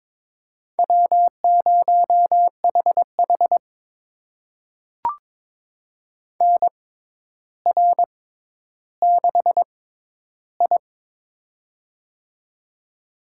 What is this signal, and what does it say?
Morse code, heard as a keyed tone, about 22 wpm, spaced as for 11 wpm, 700 Hz. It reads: W0HH NR6I
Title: USA Call Signs - 4 Characters - 6x Character Spacing - 22wpm